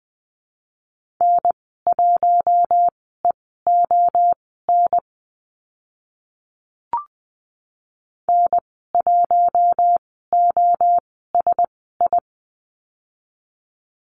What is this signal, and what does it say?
Morse code, heard as a keyed tone, about 20 wpm, spaced as for 14 wpm, 700 Hz. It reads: N1EON N1OSI
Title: 042 Lesson 8g - Number 1 - Call Signs 20wpm